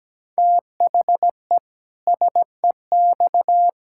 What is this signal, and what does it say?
Morse code, heard as a keyed tone, about 17 wpm, 700 Hz. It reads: THE SEX
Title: Pangram Challenge v2 - 17wpm